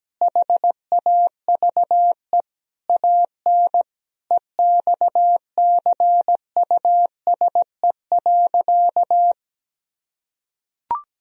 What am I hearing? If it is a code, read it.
Morse code, heard as a keyed tone, about 17 wpm, 700 Hz. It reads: HAVE AN EXCUSE.
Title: Sentences from Top 1000 Words - Review Part 4 - 17wpm